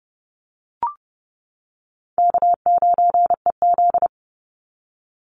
Morse code, heard as a keyed tone, about 30 wpm, 700 Hz. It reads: K9EZ
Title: NA SOTA Chaser Callsigns 30wpm